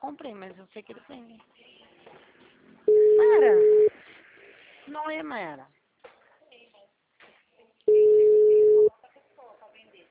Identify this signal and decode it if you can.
Ringback tone (Europe, 425 Hz), with speech.